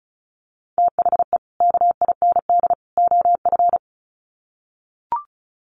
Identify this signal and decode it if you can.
Morse code, heard as a keyed tone, about 34 wpm, 700 Hz. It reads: THE KIND OF
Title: Sets of 3 Words 35wpm